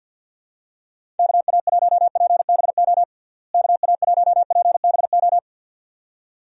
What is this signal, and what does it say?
Morse code, heard as a keyed tone, about 52 wpm, 700 Hz. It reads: KA1PBO KA1PBO
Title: USA Call Signs - 6 Characters - Repeated 2x - 50wpm